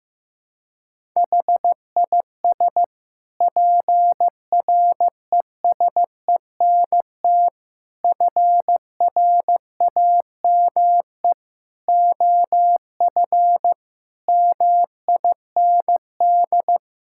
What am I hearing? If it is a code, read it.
Morse code, heard as a keyed tone, about 15 wpm, 700 Hz. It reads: HIS PRESENT FRAME OF MIND
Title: Sentences from Top 1000 Words - Review Part 6 - 15wpm